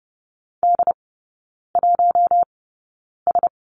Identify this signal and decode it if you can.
Morse code, heard as a keyed tone, about 30 wpm, spaced as for 9 wpm, 700 Hz. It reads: D1S
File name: Lesson 17e - Letter u - Three Characters 30wpm